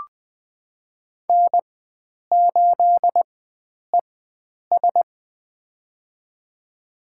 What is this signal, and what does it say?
Morse code, heard as a keyed tone, about 20 wpm, spaced as for 9 wpm, 700 Hz. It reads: N8ES
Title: USA Call Signs - 4 Characters - 4x Character Spacing - 20wpm